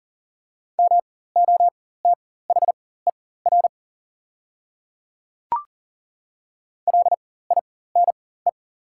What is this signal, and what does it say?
Morse code, heard as a keyed tone, about 40 wpm, spaced as for 19 wpm, 690 Hz. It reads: MOTHER LINE
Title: 4x Character Spacing - Top 200 Words - 40wpm